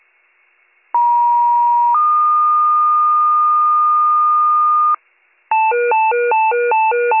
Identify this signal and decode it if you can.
Two-tone page, heard as a one-second tone, then a three-second tone, about 950 then 1220 Hz, with noise. It is heard from 0.9 to 5.0 s.